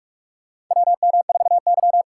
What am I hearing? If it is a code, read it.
Morse code, heard as a keyed tone, about 44 wpm, 700 Hz. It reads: WM4Y
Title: USA Call Signs - 4 Characters - 2x Character Spacing - 45wpm